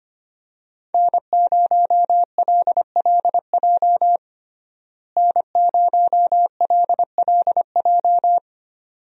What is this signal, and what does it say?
Morse code, heard as a keyed tone, about 25 wpm, 700 Hz. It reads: N0LLJ N0LLJ
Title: USA Call Signs - 5 Characters - Repeated 2x - 25wpm